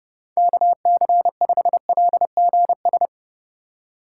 Morse code, heard as a keyed tone, about 29 wpm, 700 Hz. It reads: KC5LGS